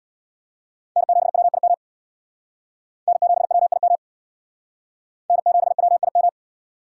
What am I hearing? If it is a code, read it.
Morse code, heard as a keyed tone, about 61 wpm, 700 Hz. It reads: N7WIM N7WIM N7WIM
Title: Speed Racing - USA 1x3 Call Signs 50wpm